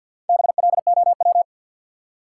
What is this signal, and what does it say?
Morse code, heard as a keyed tone, about 48 wpm, 700 Hz. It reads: BLOW
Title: Four Letter Words - 50wpm